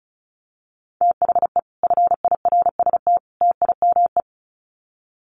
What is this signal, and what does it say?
Morse code, heard as a keyed tone, about 34 wpm, 700 Hz. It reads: THE FIRST TIME
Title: Sets of 3 Words 35wpm